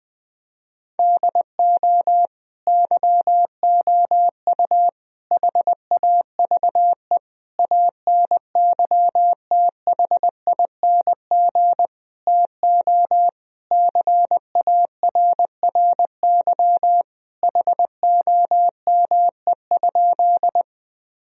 Morse code, heard as a keyed tone, about 20 wpm, 700 Hz. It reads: DO YOU HAVE ANYTHING TO CARRY HOME?
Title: Sentences from Top 1000 Words - Review Part 1 - 20wpm